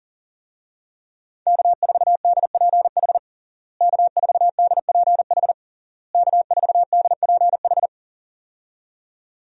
Morse code, heard as a keyed tone, about 40 wpm, 700 Hz. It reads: K4DPH K4DPH K4DPH
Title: USA Call Signs - 5 Characters - Repeated 3x - 40wpm